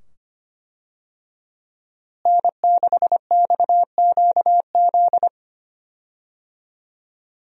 Morse code, heard as a keyed tone, about 25 wpm, 700 Hz. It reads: N6XQZ
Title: USA Call Signs - 5 Characters - Repeated 2x - 25wpm